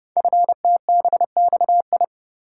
Morse code, heard as a keyed tone, about 30 wpm, 700 Hz. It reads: FTBXI
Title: Code Groups - 5 Letters - 30wpm